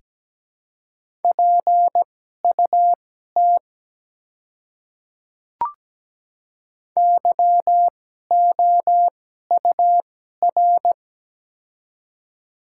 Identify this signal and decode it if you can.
Morse code, heard as a keyed tone, about 17 wpm, spaced as for 12 wpm, 700 Hz. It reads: PUT YOUR
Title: 2x Character Spacing - Top 200 Words - 17wpm